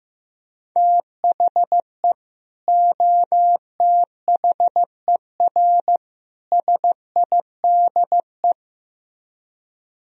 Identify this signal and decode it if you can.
Morse code, heard as a keyed tone, about 15 wpm, 700 Hz. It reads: THE OTHER SIDE
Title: Speed-Racing - Sets of 3 Words 15wpm